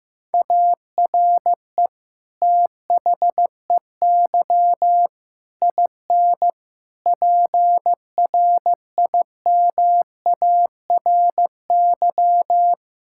Morse code, heard as a keyed tone, about 16 wpm, 700 Hz. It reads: ARE THEY IN PRIMARY
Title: Sentences from Top 1000 Words - Review Part 5 - 15wpm